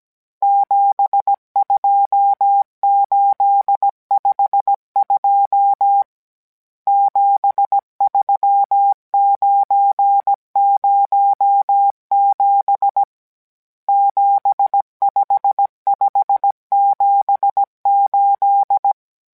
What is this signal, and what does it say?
Morse code, heard as a keyed tone, about 17 wpm, 800 Hz. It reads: 72852 73907 75578